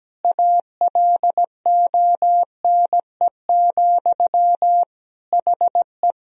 Morse code, heard as a keyed tone, about 17 wpm, 700 Hz. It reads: ALONE, HE
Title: Sentences from Top 1000 Words - Review Part 2 - 17wpm